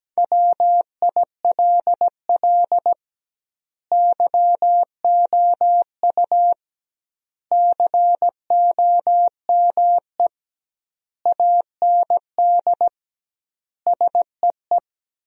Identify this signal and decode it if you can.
Morse code, heard as a keyed tone, about 17 wpm, 700 Hz. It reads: WILL YOU COME AND SEE
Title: 2x Word-Spacing - Sentences from Top 100 Words - 17wpm